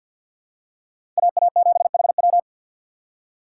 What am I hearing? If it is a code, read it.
Morse code, heard as a keyed tone, about 49 wpm, 700 Hz. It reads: AA7HW
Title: USA Call Signs - 5 Characters - 4x Character Spacing - 50wpm